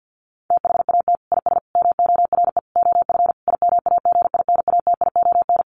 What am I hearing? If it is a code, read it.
Morse code, heard as a keyed tone, about 50 wpm, 700 Hz. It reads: THAT IS MORE OF IMAGINATION
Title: Sentences from Top 1300 Words 50wpm